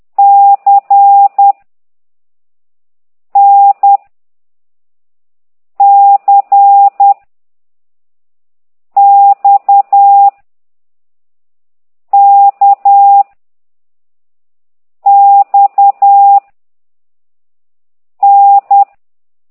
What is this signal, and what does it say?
Morse code, heard as a keyed tone, about 10 wpm, 800 Hz. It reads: C N C X K X N
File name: ct2idw / exerc4